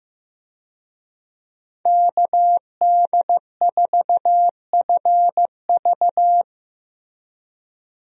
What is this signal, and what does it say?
Morse code, heard as a keyed tone, about 15 wpm, 700 Hz. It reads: KD4FV
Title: USA Call Signs - 5 Characters - Repeated 2x - 15wpm